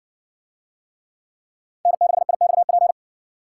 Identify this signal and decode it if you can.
Morse code, heard as a keyed tone, about 60 wpm, 700 Hz. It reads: N6IXP